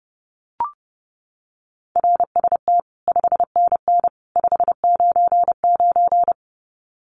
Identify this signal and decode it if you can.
Morse code, heard as a keyed tone, about 28 wpm, 700 Hz. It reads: RST 5NN 599
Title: Common QSO Phrases 30wpm